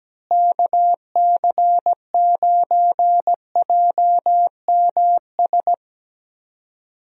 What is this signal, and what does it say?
Morse code, heard as a keyed tone, about 17 wpm, 700 Hz. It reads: KC9JMS
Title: USA Call Signs - 6 Characters - Repeated 2x - 17wpm